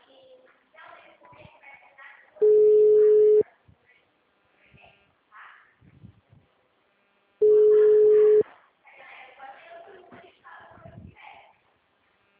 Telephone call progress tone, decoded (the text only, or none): ringback tone